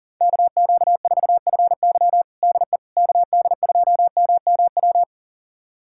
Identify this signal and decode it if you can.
Morse code, heard as a keyed tone, about 40 wpm, 700 Hz. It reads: KQ4FY DE KD2MMW
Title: USA Callsign DE Callsign 40wpm